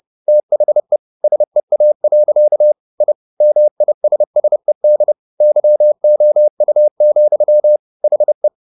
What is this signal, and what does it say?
Morse code, heard as a keyed tone, about 30 wpm, 600 Hz. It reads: THE SEA. I MISSED YOU, HE